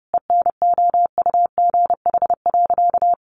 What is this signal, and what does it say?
Morse code, heard as a keyed tone, about 28 wpm, 700 Hz. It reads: ENOUGH.
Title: Sentences from Top 400 Words 30wpm